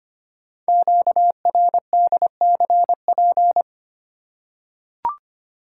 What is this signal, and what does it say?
Morse code, heard as a keyed tone, about 25 wpm, 700 Hz. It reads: QRDCP
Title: Code Groups - 5 Letter:Numbers - 25wpm